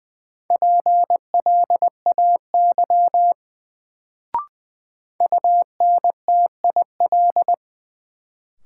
Morse code, heard as a keyed tone, about 20 wpm, 700 Hz. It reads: PLAY UNTIL
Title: Sets of 5 Words Encore - New Word Drill - 20wpm